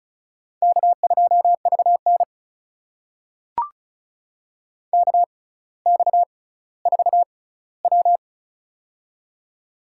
Morse code, heard as a keyed tone, about 34 wpm, spaced as for 18 wpm, 700 Hz. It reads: K2VN KX4W
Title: USA Call Signs - 4 Characters - 6x Character Spacing - 35wpm